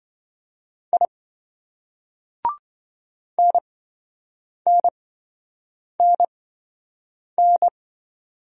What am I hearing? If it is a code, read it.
Morse code, heard as a keyed tone, about 27 wpm, spaced as for 7 wpm, 700 Hz. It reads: I NNNN